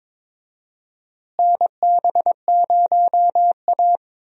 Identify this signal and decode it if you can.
Morse code, heard as a keyed tone, about 21 wpm, 700 Hz. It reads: NB0A
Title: USA Call Signs - 4 Characters - Repeated 3x - 22wpm